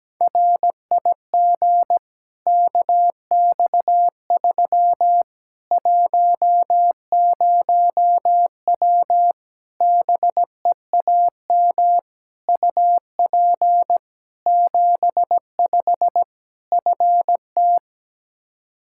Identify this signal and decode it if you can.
Morse code, heard as a keyed tone, about 17 wpm, 700 Hz. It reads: RIG KX3 10W BEAM UP 75 FT
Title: Practice QSO - Exchange 2 - Info - 17wpm